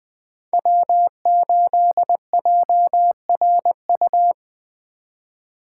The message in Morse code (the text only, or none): W8JRU